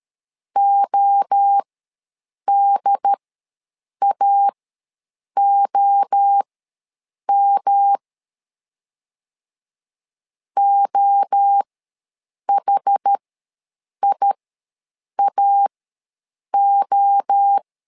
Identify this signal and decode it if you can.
Morse code, heard as a keyed tone, about 13 wpm, spaced as for 7 wpm, 800 Hz. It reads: ODAOM OHIAO
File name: LEZIONI / 3 MWRO / 3 MWRO ws 12wpm 1914